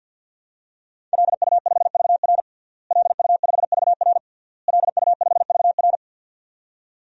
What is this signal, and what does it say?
Morse code, heard as a keyed tone, about 52 wpm, 700 Hz. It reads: LU5VR LU5VR LU5VR
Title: DXCC-100 Call Signs - 5 Characters - Repeated 3x - 50wpm